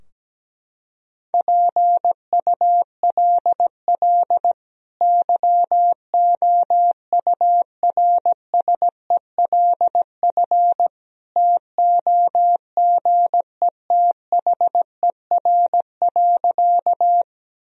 Morse code, heard as a keyed tone, about 17 wpm, 700 Hz. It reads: PULL YOURSELF TOGETHER.
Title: Sentences from Top 1000 Words - Review Part 2 - 17wpm